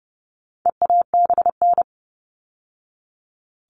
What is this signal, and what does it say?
Morse code, heard as a keyed tone, about 30 wpm, 700 Hz. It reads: EABN
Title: All Cut Numbers - 4-Digit 30wpm